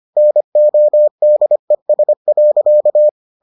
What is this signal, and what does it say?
Morse code, heard as a keyed tone, about 25 wpm, 600 Hz. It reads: NODES.